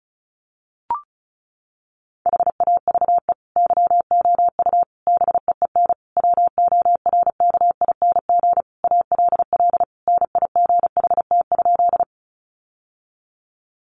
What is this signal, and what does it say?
Morse code, heard as a keyed tone, about 34 wpm, 700 Hz. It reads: HAVE YOU BEEN WORKING ALL NIGHT?
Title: Sentences from Top 700 Words 35wpm